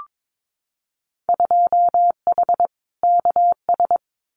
Morse code, heard as a keyed tone, about 22 wpm, 700 Hz. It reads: 2H KS
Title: Winter Field Day Exchanges Encore 22wpm